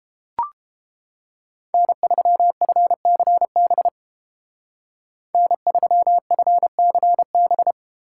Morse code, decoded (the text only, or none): N3FCB N3FCB